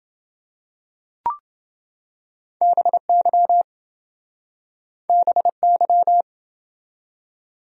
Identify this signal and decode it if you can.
Morse code, heard as a keyed tone, about 27 wpm, 700 Hz. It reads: BY BY